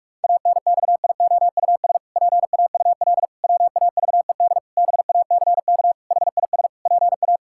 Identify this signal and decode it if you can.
Morse code, heard as a keyed tone, about 46 wpm, 700 Hz. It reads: ANXIOUS PAUL WAVED BACK HIS PA
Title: Pangram Challenge v2 - 45wpm